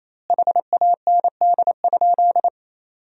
Morse code, heard as a keyed tone, about 28 wpm, 700 Hz. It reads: HAND?
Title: 2x Word-Spacing - Sentences from Top 200 Words - 28wpm